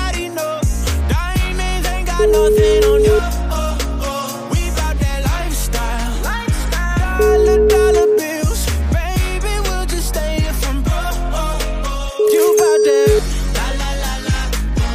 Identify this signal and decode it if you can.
Ringback tone (Europe, 425 Hz), with noise.